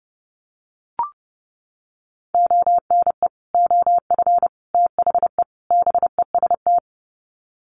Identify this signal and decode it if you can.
Morse code, heard as a keyed tone, about 30 wpm, 700 Hz. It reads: ONE OF THE BEST
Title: Sets of 4 Words 30wpm